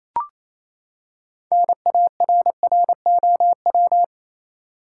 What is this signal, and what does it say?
Morse code, heard as a keyed tone, about 27 wpm, 700 Hz. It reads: NARROW
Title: Six Letter Words - 28wpm